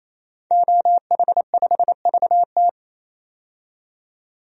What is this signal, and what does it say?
Morse code, heard as a keyed tone, about 28 wpm, 700 Hz. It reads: OH5VT
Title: DXCC-100 Call Signs - 5 Characters - 28wpm